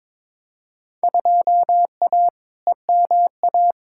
Morse code, heard as a keyed tone, about 21 wpm, 700 Hz. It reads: 2A EMA
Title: Field Day Exchanges Encore - 22wpm